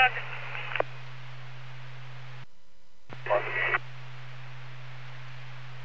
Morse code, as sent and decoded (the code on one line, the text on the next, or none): none